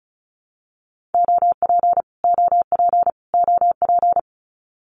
Morse code, heard as a keyed tone, about 34 wpm, 700 Hz. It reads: OP OP OP
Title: Abbreviations Repeated 3x - 35wpm